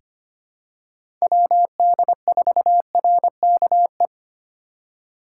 Morse code, heard as a keyed tone, about 25 wpm, 700 Hz. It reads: WD4RKE
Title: USA Call Signs - 6 Characters - 3x Character Spacing - 25wpm